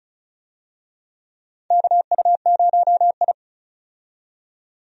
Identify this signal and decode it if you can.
Morse code, heard as a keyed tone, about 36 wpm, 700 Hz. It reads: KU0I
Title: USA Call Signs - 4 Characters - 35wpm